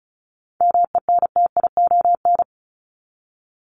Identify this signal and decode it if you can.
Morse code, heard as a keyed tone, about 34 wpm, 700 Hz. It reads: MENTION
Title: ED Suffix - Top 100 Words with Base Words 35wpm